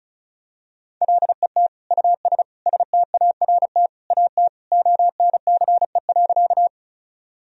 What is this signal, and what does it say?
Morse code, heard as a keyed tone, about 34 wpm, 700 Hz. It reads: LET US START AT ONCE.
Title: Sentences from Top 1000 Words - Review Part 1 - 35wpm